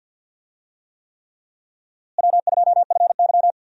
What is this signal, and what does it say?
Morse code, heard as a keyed tone, about 50 wpm, 700 Hz. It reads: W2FY
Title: USA Call Signs - 4 Characters - Repeated 3x - 50wpm